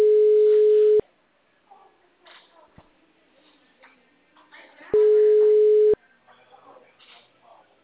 Ringback tone (Europe, 425 Hz).